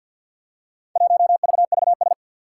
Morse code, heard as a keyed tone, about 52 wpm, 700 Hz. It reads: 1VVS
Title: Four Challenging Letter-Number Combinations 50wpm